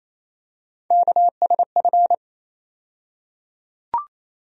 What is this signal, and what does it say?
Morse code, heard as a keyed tone, about 28 wpm, 700 Hz. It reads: KSF
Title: USA Call Sign Postfixes - 3 Letters 28wpm